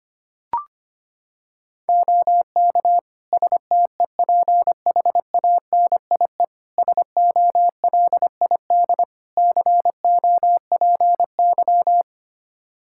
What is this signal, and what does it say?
Morse code, heard as a keyed tone, about 25 wpm, 700 Hz. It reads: OK STEPHANIE SOLID COPY
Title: Practice QSO - Exchange 3 - More Info - 25wpm